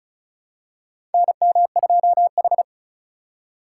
Morse code, heard as a keyed tone, about 34 wpm, 690 Hz. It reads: NM2H
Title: USA Call Signs - 3x Character Spacing - 35wpm